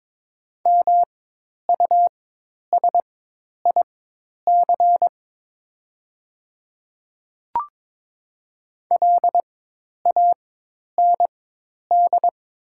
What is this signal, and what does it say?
Morse code, heard as a keyed tone, about 22 wpm, spaced as for 10 wpm, 700 Hz. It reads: MUSIC LAND
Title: 4x Character Spacing - Top 500 Words - 22wpm